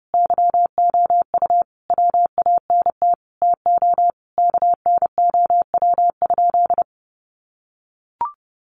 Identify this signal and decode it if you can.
Morse code, heard as a keyed tone, about 30 wpm, 700 Hz. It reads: YOU WANT TO KNOW?
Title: Sentences from Top 200 Words 30wpm